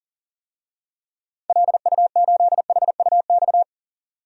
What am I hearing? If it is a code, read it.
Morse code, heard as a keyed tone, about 40 wpm, 700 Hz. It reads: LU8HUX
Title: DXCC-100 Call Signs - 6 Characters - 40wpm